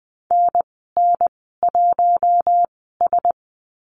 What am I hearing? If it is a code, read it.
Morse code, heard as a keyed tone, about 20 wpm, spaced as for 14 wpm, 700 Hz. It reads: NN1S
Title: 076 Lesson 13g - Letter L - Call Signs 20wpm